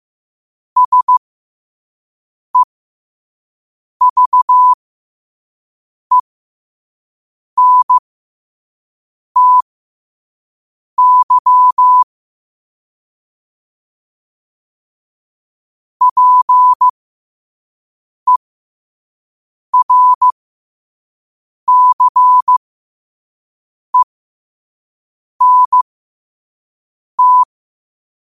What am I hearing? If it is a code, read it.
Morse code, heard as a keyed tone, about 15 wpm, spaced as for 5 wpm, 1000 Hz. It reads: SEVENTY PERCENT
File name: qotdfiles / 5wpm